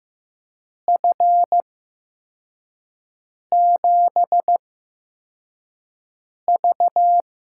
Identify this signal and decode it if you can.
Morse code, heard as a keyed tone, about 15 wpm, spaced as for 4 wpm, 700 Hz. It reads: F7V
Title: USA Call Signs - 6 Characters - 8x Character Spacing - 15wpm